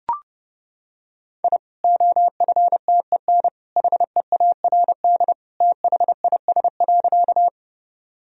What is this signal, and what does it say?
Morse code, heard as a keyed tone, about 29 wpm, 700 Hz. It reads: I OFTEN HEARD THIS.
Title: Sentences from Top 1000 Words - Review Part 1 - 30wpm